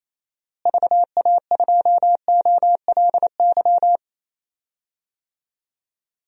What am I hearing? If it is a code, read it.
Morse code, heard as a keyed tone, about 28 wpm, 700 Hz. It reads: VA2OLY
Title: DXCC-100 Call Signs - 6 Characters - Repeated 2x - 28wpm